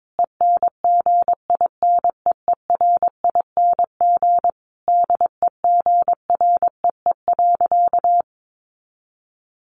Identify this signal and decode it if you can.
Morse code, heard as a keyed tone, about 22 wpm, 700 Hz. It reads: ENGINEERING DEGREE.
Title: Sentences from Top 1500 Words 22wpm